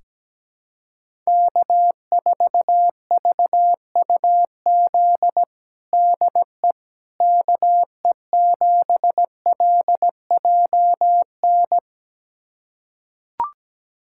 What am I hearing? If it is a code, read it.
Morse code, heard as a keyed tone, about 17 wpm, 700 Hz. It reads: K4VUZ DE KE7LJN
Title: USA Callsign DE Callsign 17wpm